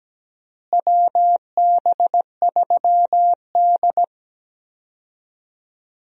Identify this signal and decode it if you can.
Morse code, heard as a keyed tone, about 17 wpm, 700 Hz. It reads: WB3D
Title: USA Call Signs - 4 Characters - Repeated 3x - 17wpm